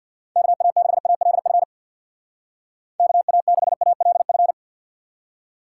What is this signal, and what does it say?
Morse code, heard as a keyed tone, about 62 wpm, 700 Hz. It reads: KA6ALF KA6ALF
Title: Speed Racing - USA 2x3 Call Signs 50wpm